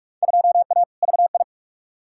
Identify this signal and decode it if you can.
Morse code, heard as a keyed tone, about 46 wpm, 700 Hz. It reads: 2A VI
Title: Field Day Exchanges - 45wpm